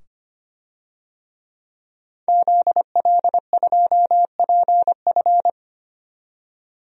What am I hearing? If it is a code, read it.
Morse code, heard as a keyed tone, about 25 wpm, 700 Hz. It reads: ZL2PF
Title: DXCC-100 Call Signs - 5 Characters - 25wpm